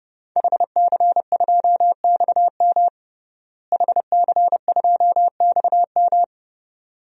Morse code, heard as a keyed tone, about 30 wpm, 700 Hz. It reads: HC2XM HC2XM
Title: DXCC-100 Call Signs - 5 Characters - Repeated 2x - 30wpm